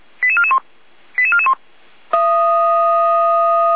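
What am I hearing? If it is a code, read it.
Five-tone selective call (ZVEI), heard as quick runs of five tones, tones of about 70 ms. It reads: 88441 88441
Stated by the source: ZVEI five-tone call 88519, received by SDR.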